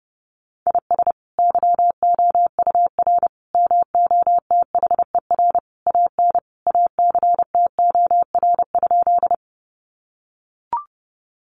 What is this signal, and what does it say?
Morse code, heard as a keyed tone, about 30 wpm, 700 Hz. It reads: IS YOUR MOTHER AN ACTOR?
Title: Sentences from Top 1500 Words 30wpm